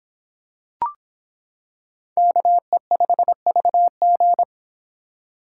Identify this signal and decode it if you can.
Morse code, heard as a keyed tone, about 25 wpm, 700 Hz. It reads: KE5VG